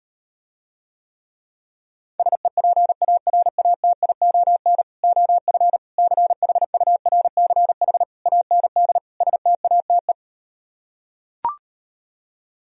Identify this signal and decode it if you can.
Morse code, heard as a keyed tone, about 38 wpm, 700 Hz. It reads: SEPARATION OF CHURCH AND STATE